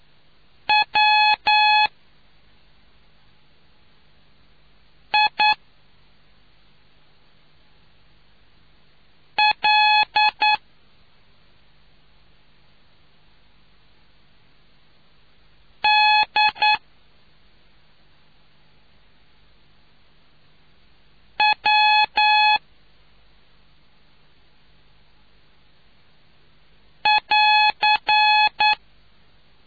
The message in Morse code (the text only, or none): WILDW<AR>